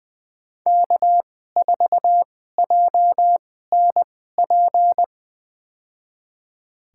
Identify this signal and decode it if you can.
Morse code, heard as a keyed tone, about 20 wpm, spaced as for 14 wpm, 700 Hz. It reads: K4JNP
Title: USA Call Signs - 5 Characters - 2x Character Spacing - 20wpm